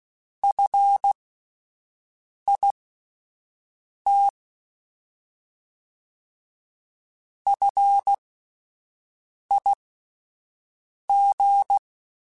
Morse code, heard as a keyed tone, about 16 wpm, spaced as for 5 wpm, 800 Hz. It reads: FIT FIG
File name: Alphabetical / Alphabetical mp3s / Fs @5